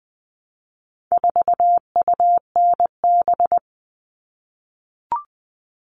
Morse code, heard as a keyed tone, about 20 wpm, 700 Hz. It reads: 4UNB
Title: All Cut Numbers - 4-Digit 20wpm